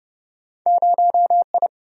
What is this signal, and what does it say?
Morse code, heard as a keyed tone, about 30 wpm, 700 Hz. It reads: 0 I
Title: Single-Letter Single-Number 30wpm